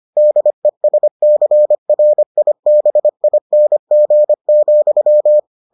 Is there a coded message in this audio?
Yes: Morse code from 0.2 to 5.4 s, heard as a keyed tone, about 25 wpm, 600 Hz. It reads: DESCRIBING,